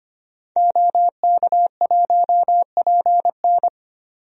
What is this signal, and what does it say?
Morse code, heard as a keyed tone, about 25 wpm, 700 Hz. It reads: OK1PN
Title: DXCC-100 Call Signs - 5 Characters - 25wpm